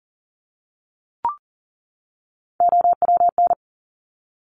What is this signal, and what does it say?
Morse code, heard as a keyed tone, about 40 wpm, 690 Hz. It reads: OWN
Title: Top Three Letter Combinations in Words 40wpm